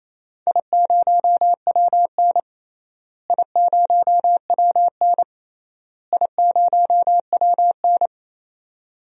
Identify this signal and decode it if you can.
Morse code, heard as a keyed tone, about 28 wpm, 700 Hz. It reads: I0WN I0WN I0WN
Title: DXCC-100 Call Signs - 4 Characters - Repeated 3x - 28wpm